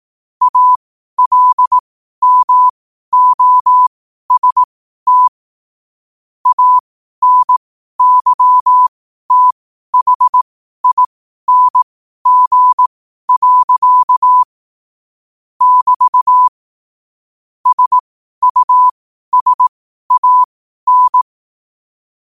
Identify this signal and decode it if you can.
Morse code, heard as a keyed tone, about 18 wpm, spaced as for 12 wpm, 1000 Hz. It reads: ALMOST ANYTHING. <BT> SUSAN
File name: qotdfiles / 12wpm